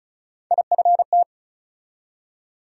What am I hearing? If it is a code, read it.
Morse code, heard as a keyed tone, about 36 wpm, 700 Hz. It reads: IFT
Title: USA Call Sign Postfixes - 3 Letters 35wpm